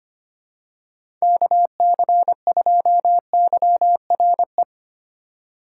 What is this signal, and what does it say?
Morse code, heard as a keyed tone, about 25 wpm, 700 Hz. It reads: KC2YRE
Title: USA Call Signs - 6 Characters - 3x Character Spacing - 25wpm